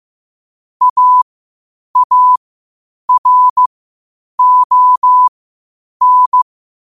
Morse code, heard as a keyed tone, about 15 wpm, spaced as for 8 wpm, 1000 Hz. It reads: AARON
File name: qotdfiles / 8wpm